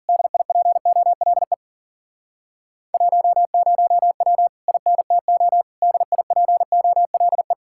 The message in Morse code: DIPOLE 10W INTO DIPOLE

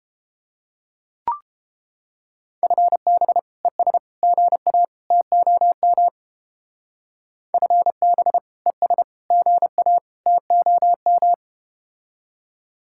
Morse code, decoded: FB ES GA TOM FB ES GA TOM

